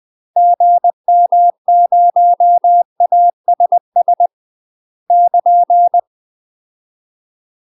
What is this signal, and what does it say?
Morse code, heard as a keyed tone, about 20 wpm, 700 Hz. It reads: GM0ASS <KN>
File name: UK MASTER QSOs TRAINING / 20WPM / UK MASTER QSOs 0025